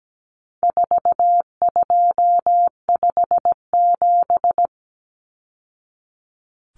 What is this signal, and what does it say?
Morse code, heard as a keyed tone, about 17 wpm, 700 Hz. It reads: 4257
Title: Numbers - 4-Digit 17wpm